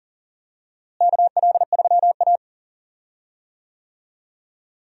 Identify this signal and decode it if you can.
Morse code, heard as a keyed tone, about 37 wpm, 700 Hz. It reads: KL3A
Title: USA Call Signs - 4 Characters - Repeated 2x - 40wpm